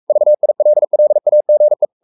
Morse code, heard as a keyed tone, about 44 wpm, 600 Hz. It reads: VILLAGE
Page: Romanen Never af Ken Follet på engelsk.